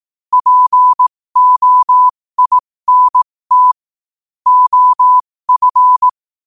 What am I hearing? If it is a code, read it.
Morse code, heard as a keyed tone, about 18 wpm, spaced as for 15 wpm, 1000 Hz. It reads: POINT OF
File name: 15wpm